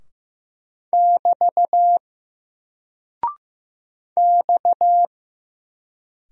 Morse code, heard as a keyed tone, about 15 wpm, 700 Hz. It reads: <BT> X
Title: Single Character (Letter:Number:Punctuation:Prosign) Alt 15wpm